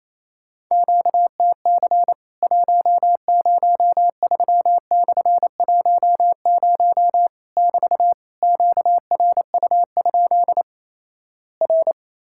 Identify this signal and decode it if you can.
Morse code, heard as a keyed tone, about 28 wpm, 700 Hz. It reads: QTC 103/10 <BT> QRU? R
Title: Worked All Europe DX Contest - QTC Exchange - 28wpm